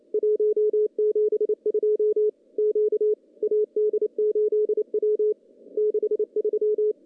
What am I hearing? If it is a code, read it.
Morse code, heard as a keyed tone, about 28 wpm, 410 Hz. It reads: 172 Q AD8W 63
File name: contests / CW-SS-k5tr- -0200